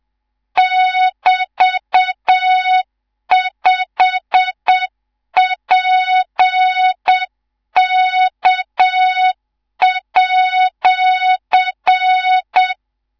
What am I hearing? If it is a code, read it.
Morse code, heard as a keyed tone, about 7 wpm, 3700 Hz, with noise. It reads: <BT>5PK@
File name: Morze-HA2MN / hang / 19 GYAKORLAT